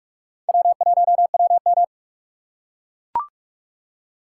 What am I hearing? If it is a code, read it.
Morse code, heard as a keyed tone, about 44 wpm, 700 Hz. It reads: W1WM
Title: USA Call Signs - 4 Characters - 4x Character Spacing - 45wpm